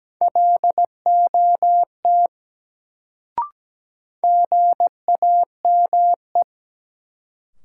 Morse code, heard as a keyed tone, about 17 wpm, 700 Hz. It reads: LOT GAME
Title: Sentences from Top 500 Words - New Word Drill - 17wpm